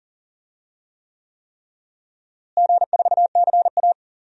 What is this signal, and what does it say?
Morse code, heard as a keyed tone, about 40 wpm, 700 Hz. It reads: G4CA
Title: DXCC-100 Call Signs - 4 Characters - Repeated 3x - 40wpm